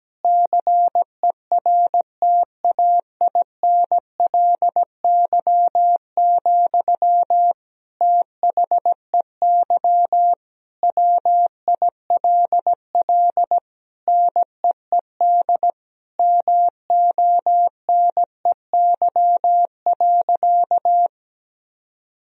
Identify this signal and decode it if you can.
Morse code, heard as a keyed tone, about 17 wpm, 700 Hz. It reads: CERTAINLY, THEY WILL NEED MONEY.
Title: Sentences from Top 1000 Words - Review Part 4 - 17wpm